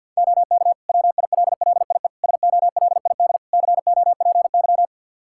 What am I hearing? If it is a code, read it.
Morse code, heard as a keyed tone, about 46 wpm, 700 Hz. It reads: OK WILLIE SOLID COPY
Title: Practice QSO - Exchange 3 - More Info - 50wpm